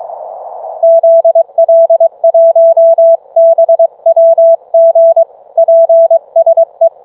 Morse code, heard as a keyed tone, about 19 wpm, 660 Hz. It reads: ZL1BWG PSE